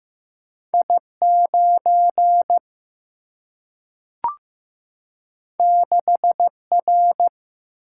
Morse code, heard as a keyed tone, about 15 wpm, 700 Hz. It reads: I9 6R
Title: Single-Letter Single-Number 15wpm